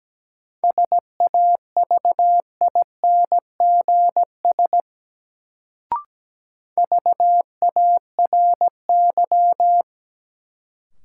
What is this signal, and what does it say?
Morse code, heard as a keyed tone, about 17 wpm, 700 Hz. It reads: SAVINGS VARY
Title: Sentences from Top 1200 Words - New Word Drill - 17wpm